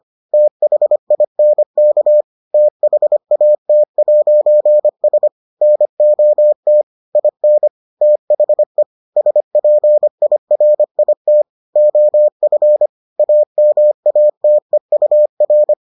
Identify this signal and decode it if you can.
Morse code, heard as a keyed tone, about 25 wpm, 600 Hz. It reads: THINK THAT'S NOT IN THE SPIRIT OF AMATEUR